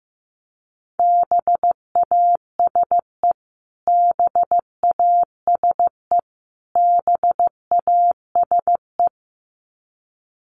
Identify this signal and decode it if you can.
Morse code, heard as a keyed tone, about 14 wpm, 700 Hz. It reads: BASE BASE BASE
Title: Top 300 Words Repeated 15wpm